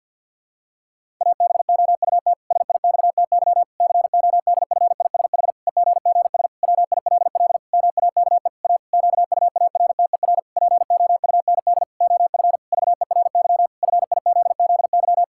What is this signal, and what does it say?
Morse code, heard as a keyed tone, about 50 wpm, 700 Hz. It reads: ABOUT SIXTY CODFISH EGGS WILL MAKE A QUARTER POUND OF VERY FIZZY
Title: Pangram Challenge v2 - 50wpm